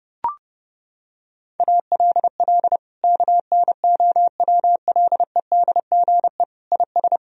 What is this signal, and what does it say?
Morse code, heard as a keyed tone, about 30 wpm, 700 Hz. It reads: ALL KNOWLEDGE IS
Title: Sentences from Top 1000 Words - Review Part 3 - 30wpm